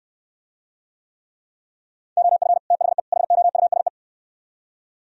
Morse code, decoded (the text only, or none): OF THE HOUSE